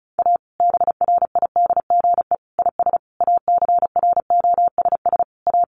Morse code, heard as a keyed tone, about 34 wpm, 700 Hz. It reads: A BRIDGE IS ACROSS A
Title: Sentences from Top 1200 Words 35wpm